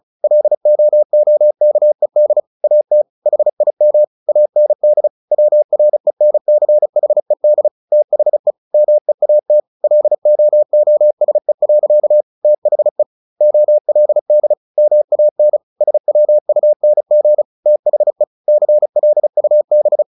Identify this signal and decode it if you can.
Morse code, heard as a keyed tone, about 34 wpm, 600 Hz. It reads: LOOKED AT HIM AND WRENCHED THE MEAT LOOSE. THE OLD MAN SWUNG THE CLUB
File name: the old man and the sea 2 35wpm